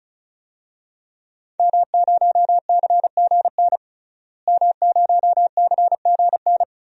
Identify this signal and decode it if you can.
Morse code, heard as a keyed tone, about 36 wpm, 700 Hz. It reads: M0CGN M0CGN
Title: DXCC-100 Call Signs - 5 Characters - Repeated 3x - 35wpm